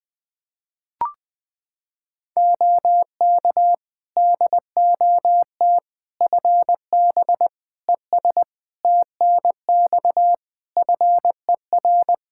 Morse code, heard as a keyed tone, about 20 wpm, 700 Hz. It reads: OK DOT FB ES TNX FER
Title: Practice QSO - Exchange 2 - Info - 20wpm